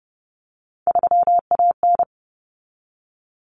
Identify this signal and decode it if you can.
Morse code, heard as a keyed tone, about 30 wpm, 700 Hz. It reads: 3AN
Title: Common Cut Numbers - 3-Digit 30wpm